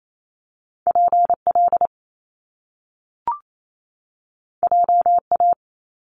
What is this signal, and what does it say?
Morse code, heard as a keyed tone, about 28 wpm, 700 Hz. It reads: PL JA
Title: Top Two Letter Combinations in Words 28wpm